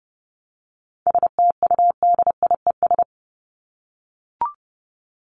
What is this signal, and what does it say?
Morse code, heard as a keyed tone, about 30 wpm, 700 Hz. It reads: STUDIES
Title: ES:S Suffix - Top 100 Words with Base Words 30wpm